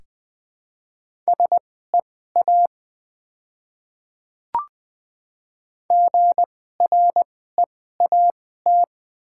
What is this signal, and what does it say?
Morse code, heard as a keyed tone, about 20 wpm, spaced as for 14 wpm, 700 Hz. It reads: SEA GREAT
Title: 2x Character Spacing - Top 500 Words - 20wpm